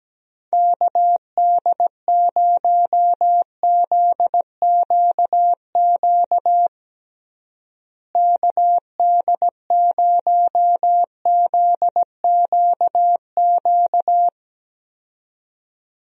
Morse code, heard as a keyed tone, about 17 wpm, 700 Hz. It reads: KD0ZQQ KD0ZQQ
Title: USA Call Signs - 6 Characters - Repeated 2x - 17wpm